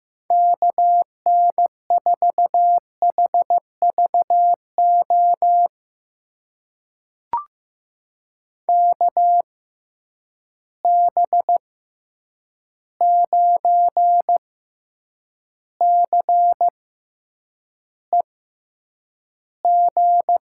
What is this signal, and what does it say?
Morse code, heard as a keyed tone, about 15 wpm, spaced as for 8 wpm, 700 Hz. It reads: KN4HVO KB9CEG
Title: USA Call Signs - 6 Characters - 6x Character Spacing - 15wpm